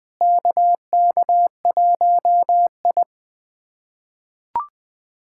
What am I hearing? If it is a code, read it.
Morse code, heard as a keyed tone, about 20 wpm, 700 Hz. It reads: KK1I
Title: USA Callsign DE Callsign - 4 Characters - 20wpm